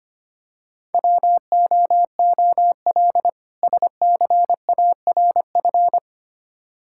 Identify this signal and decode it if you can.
Morse code, heard as a keyed tone, about 25 wpm, 700 Hz. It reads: WOOL SCARF
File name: practice / 25wpm / Difficult Words with Hints - 25wpm